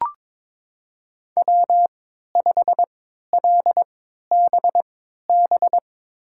Morse code, heard as a keyed tone, about 22 wpm, spaced as for 13 wpm, 700 Hz. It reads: W5LBB
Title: USA Call Signs - 5 Characters - 3x Character Spacing - 22wpm